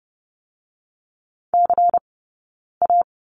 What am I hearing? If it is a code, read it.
Morse code, heard as a keyed tone, about 30 wpm, 700 Hz. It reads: C A
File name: 092 Lesson 16d - Letter c - Two Characters 30wpm